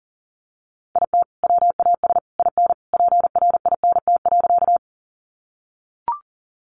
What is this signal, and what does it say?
Morse code, heard as a keyed tone, about 40 wpm, 700 Hz. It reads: IT WAS IN PRINT.
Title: Sentences from Top 1400 Words 40wpm